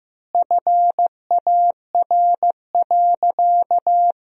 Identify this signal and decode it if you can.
Morse code, heard as a keyed tone, about 15 wpm, 700 Hz. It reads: FAR.
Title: 4x Word-Spacing - Sentences from Top 400 Words - 15wpm